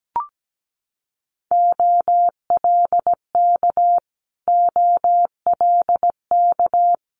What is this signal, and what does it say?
Morse code, heard as a keyed tone, about 17 wpm, 690 Hz. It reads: OLK OLK
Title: Multi-letter Phonemes Repeated 3x 17wpm